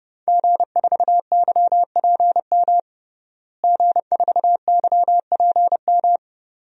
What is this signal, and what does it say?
Morse code, heard as a keyed tone, about 32 wpm, 700 Hz. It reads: G4YPM G4YPM
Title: DXCC-100 Call Signs - 5 Characters - Repeated 2x - 30wpm